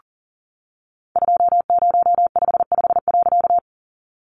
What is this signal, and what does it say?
Morse code, heard as a keyed tone, about 40 wpm, 700 Hz. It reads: 2055.
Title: A Sound Of Thunder 40wpm